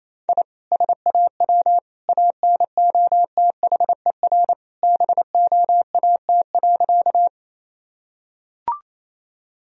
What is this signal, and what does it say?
Morse code, heard as a keyed tone, about 28 wpm, 700 Hz. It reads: I SAW ANOTHER BOAT.
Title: Sentences from Top 1000 Words - Review Part 3 - 28wpm